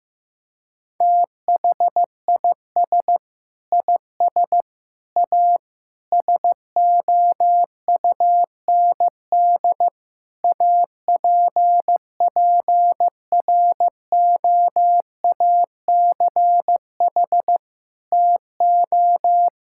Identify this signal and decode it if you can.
Morse code, heard as a keyed tone, about 15 wpm, 700 Hz. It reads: THIS IS A SOUND APPROACH TO
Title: Sentences from Top 1000 Words - Review Part 6 - 15wpm